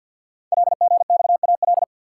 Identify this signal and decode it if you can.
Morse code, heard as a keyed tone, about 50 wpm, 700 Hz. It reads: LGXAL
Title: Code Groups - 5 Letters - 50wpm